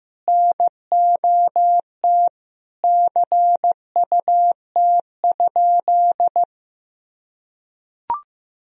Morse code, heard as a keyed tone, about 16 wpm, 700 Hz. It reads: NOT CUT?
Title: Sentences from Top 1000 Words - Review Part 1 - 15wpm